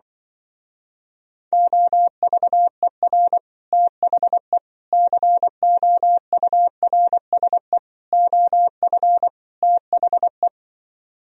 Morse code, heard as a keyed tone, about 24 wpm, 700 Hz. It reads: OVER THE COURSE OF THE
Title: Speed-Racing - Sets of 5 Words 20wpm